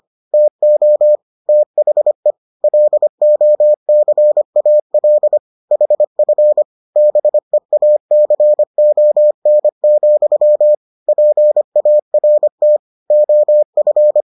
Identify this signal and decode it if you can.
Morse code, heard as a keyed tone, about 25 wpm, 600 Hz. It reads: TO THE LOCAL HF BEACON, PART OF